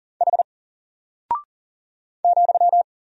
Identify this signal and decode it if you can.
Morse code, heard as a keyed tone, about 40 wpm, 700 Hz. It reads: H ,
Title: Single Character (Letter:Number:Punctuation:Prosign) Alt 40wpm